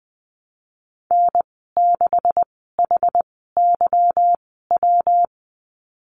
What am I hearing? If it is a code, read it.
Morse code, heard as a keyed tone, about 20 wpm, spaced as for 14 wpm, 700 Hz. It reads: N6HYW
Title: 193 Lesson 31g - Letter v - Call Signs 20wpm